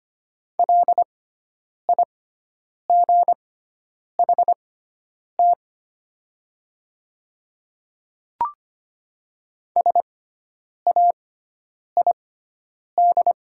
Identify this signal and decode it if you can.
Morse code, heard as a keyed tone, about 25 wpm, spaced as for 9 wpm, 690 Hz. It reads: LIGHT SAID
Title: 6x Character Spacing - Top 200 Words - 25wpm